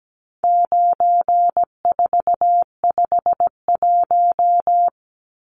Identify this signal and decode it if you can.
Morse code, heard as a keyed tone, about 17 wpm, 700 Hz. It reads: 9451
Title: Numbers - 4-Digit 17wpm